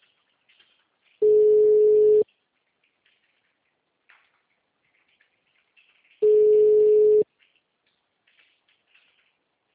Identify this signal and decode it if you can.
Ringback tone (Europe, 425 Hz).
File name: out- -2044- - - .714